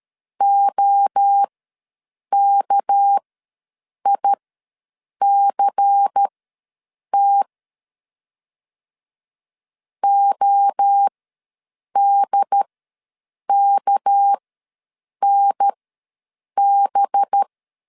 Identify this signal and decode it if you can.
Morse code, heard as a keyed tone, about 13 wpm, spaced as for 7 wpm, 800 Hz. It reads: OKICT ODKNB
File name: LEZIONI / 6 PKX0 / 6 PK ws 12wpm 3116